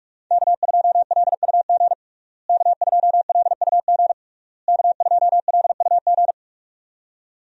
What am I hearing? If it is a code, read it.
Morse code, heard as a keyed tone, about 41 wpm, 700 Hz. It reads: K2LUG K2LUG K2LUG
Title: USA Call Signs - 5 Characters - Repeated 3x - 45wpm